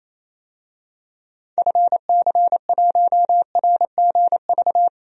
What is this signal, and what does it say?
Morse code, heard as a keyed tone, about 27 wpm, 700 Hz. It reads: FC1RGV
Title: DXCC-100 Call Signs - 6 Characters - 28wpm